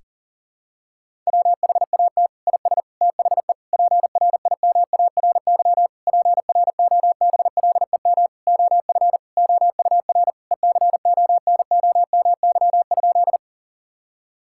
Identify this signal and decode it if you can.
Morse code, heard as a keyed tone, about 40 wpm, 700 Hz. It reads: WHAT IS THE PRIMARY PROBLEM OF OUR ECONOMY?
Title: Sentences from Top 1000 Words - Review Part 5 - 40wpm